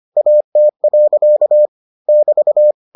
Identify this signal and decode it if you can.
Morse code, heard as a keyed tone, about 25 wpm, 600 Hz. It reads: AT. <BT>